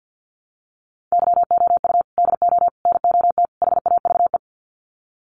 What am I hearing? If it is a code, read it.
Morse code, heard as a keyed tone, about 50 wpm, 700 Hz. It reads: YOU DO NOT HAVE
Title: Sets of 4 Words 50wpm